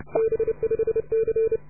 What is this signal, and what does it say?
Morse code, heard as a keyed tone, about 30 wpm, 440 Hz. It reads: B5C